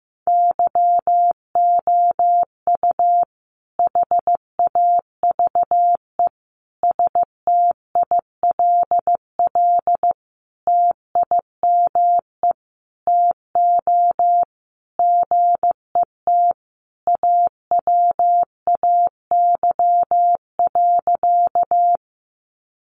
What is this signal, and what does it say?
Morse code, heard as a keyed tone, about 15 wpm, 700 Hz. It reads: YOU HAVE STILL TIME TO GET AWAY.
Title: Sentences from Top 600 Words 15wpm